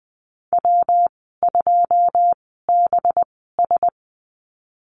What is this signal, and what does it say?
Morse code, heard as a keyed tone, about 19 wpm, spaced as for 14 wpm, 700 Hz. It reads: W2BS
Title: 219 Lesson 35g - Number 0 - Call Signs 20wpm